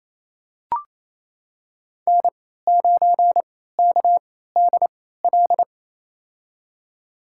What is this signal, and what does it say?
Morse code, heard as a keyed tone, about 28 wpm, spaced as for 16 wpm, 700 Hz. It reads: N9KDL
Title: USA Call Signs - 3x Character Spacing - 28wpm